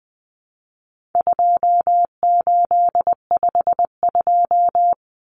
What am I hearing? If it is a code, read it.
Morse code, heard as a keyed tone, about 20 wpm, 700 Hz. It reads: 2852
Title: Common Cut Numbers - 4-Digit 20wpm